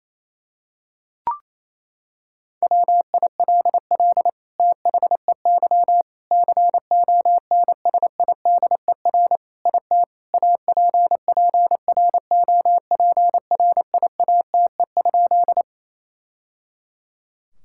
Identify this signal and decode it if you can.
Morse code, heard as a keyed tone, about 28 wpm, 700 Hz. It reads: WILL THEY CONSIDER IT APPROPRIATE?
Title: Sentences from Top 1000 Words - Review Part 5 - 28wpm